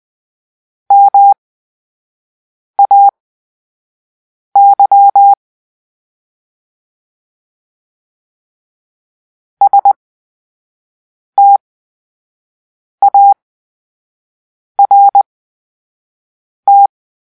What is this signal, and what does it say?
Morse code, heard as a keyed tone, about 20 wpm, spaced as for 5 wpm, 800 Hz. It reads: MAY START
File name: Grp-3 Drill-2